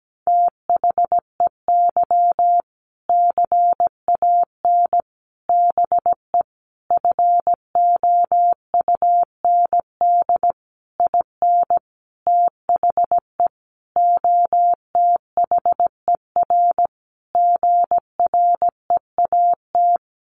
Morse code, heard as a keyed tone, about 17 wpm, 700 Hz. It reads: THEY CAN BE FOUND IN THE OTHER GREAT
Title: Sentences from Top 1500 Words 17wpm